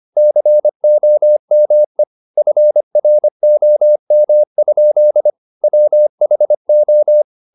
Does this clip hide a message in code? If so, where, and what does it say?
Yes: Morse code from 0.2 to 7.2 s, heard as a keyed tone, about 25 wpm, 600 Hz. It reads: COME FROM? WHO